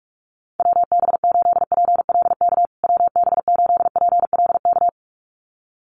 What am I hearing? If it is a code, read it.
Morse code, heard as a keyed tone, about 44 wpm, 700 Hz. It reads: WB8PLK WB8PLK
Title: USA Call Signs Repeated 2x 45wpm